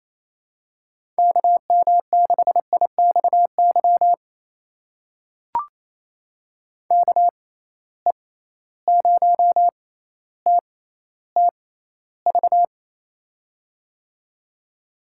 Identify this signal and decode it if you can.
Morse code, heard as a keyed tone, about 27 wpm, spaced as for 14 wpm, 700 Hz. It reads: KM6IXY KE0TTV
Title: USA Call Signs - 6 Characters - 6x Character Spacing - 28wpm